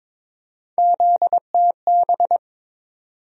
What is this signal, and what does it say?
Morse code, heard as a keyed tone, about 22 wpm, 700 Hz. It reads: ZTB
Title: USA Call Sign Postfixes - 3 Letters 22wpm